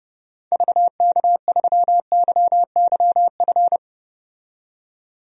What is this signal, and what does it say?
Morse code, heard as a keyed tone, about 30 wpm, 700 Hz. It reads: VK3YYF
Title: DXCC-100 Call Signs - 6 Characters - 30wpm